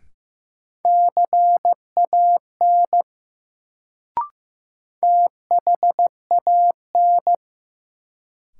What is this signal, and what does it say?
Morse code, heard as a keyed tone, about 15 wpm, 700 Hz. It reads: CAN THAN
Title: Sentences from Top 100 Words - New Word Drill - 15wpm